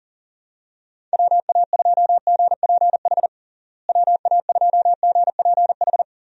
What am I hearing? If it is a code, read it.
Morse code, heard as a keyed tone, about 40 wpm, 700 Hz. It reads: WA2GPH WA2GPH
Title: USA Call Signs - 6 Characters - Repeated 3x - 40wpm